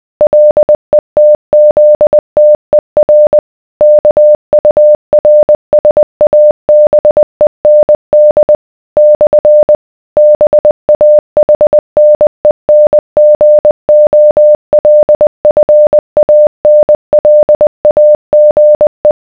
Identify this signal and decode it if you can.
Morse code, heard as a keyed tone, about 20 wpm, 600 Hz. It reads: LETZTER KURSABEND / BAHNENGOLFANLAGE